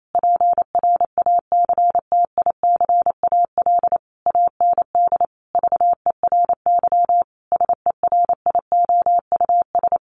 Morse code, heard as a keyed tone, about 28 wpm, 700 Hz. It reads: PRACTICAL AND VERY SERIOUS
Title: Sentences from Top 1400 Words 28wpm